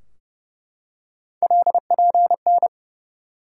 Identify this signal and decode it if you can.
Morse code, heard as a keyed tone, about 31 wpm, 700 Hz. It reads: LPN
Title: K1USN SST QSO Practice 30wpm